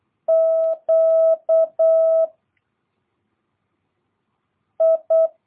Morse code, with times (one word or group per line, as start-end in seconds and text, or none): 0.3-2.2 Q
4.8-5.3 I